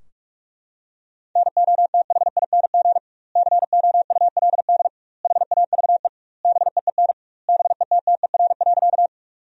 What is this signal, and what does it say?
Morse code, heard as a keyed tone, about 46 wpm, 700 Hz. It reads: NOTHING COULD HAVE BEEN BETTER.
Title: 2x Word-Spacing - Sentences from Top 500 Words - 45wpm